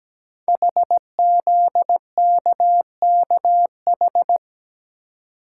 Morse code, heard as a keyed tone, about 17 wpm, 700 Hz. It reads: HZKKH
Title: Code Groups - 5 Letters - 17wpm